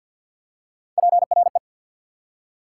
Morse code, heard as a keyed tone, about 48 wpm, 690 Hz. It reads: PRE
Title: Prefixes and Suffixes 50wpm